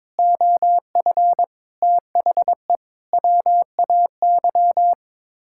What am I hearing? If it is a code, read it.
Morse code, heard as a keyed tone, about 22 wpm, 700 Hz. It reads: OF THE WAY
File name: practice / 22wpm / Sets of 7 Words - 22wpm